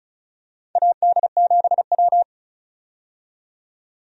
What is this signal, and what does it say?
Morse code, heard as a keyed tone, about 36 wpm, 700 Hz. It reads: AD7W
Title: USA Call Signs - 4 Characters - 35wpm